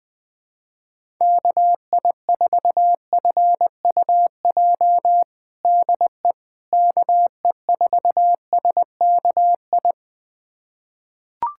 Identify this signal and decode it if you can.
Morse code, heard as a keyed tone, about 21 wpm, 700 Hz. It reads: KI4FUJ DE KE4SKI
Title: USA Callsign DE Callsign - 6 Characters - 20wpm